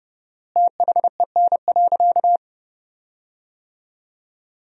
Morse code, heard as a keyed tone, about 32 wpm, 700 Hz. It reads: THEN.
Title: 4x Word-Spacing - Sentences from Top 200 Words - 30wpm